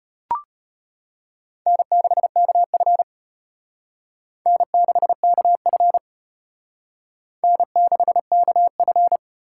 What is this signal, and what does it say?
Morse code, heard as a keyed tone, about 31 wpm, 700 Hz. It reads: N6KF N6KF N6KF
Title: Speed Racing - USA 1x2 Call Signs 25wpm